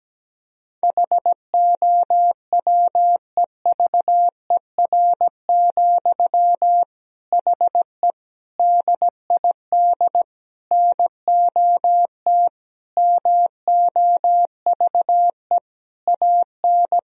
Morse code, heard as a keyed tone, about 17 wpm, 700 Hz. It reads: HOWEVER, HE DID NOT MOVE AN
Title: Sentences from Top 1000 Words - Review Part 3 - 17wpm